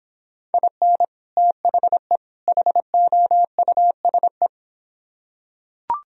Morse code, heard as a keyed tone, about 26 wpm, 700 Hz. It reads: IN THE HOUSE